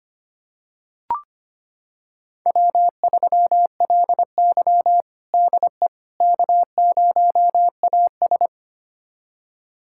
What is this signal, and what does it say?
Morse code, heard as a keyed tone, about 25 wpm, 700 Hz. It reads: W3LY DE K0AS
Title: USA Callsign DE Callsign - 4 Characters - 25wpm